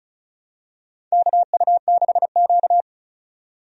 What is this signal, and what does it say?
Morse code, heard as a keyed tone, about 36 wpm, 700 Hz. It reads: KU6Q
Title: USA Call Signs - 4 Characters - 35wpm